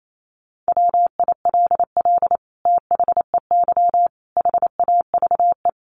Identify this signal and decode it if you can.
Morse code, heard as a keyed tone, about 28 wpm, 700 Hz. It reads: WILL THEY HAVE
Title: Sentences from Top 1500 Words 28wpm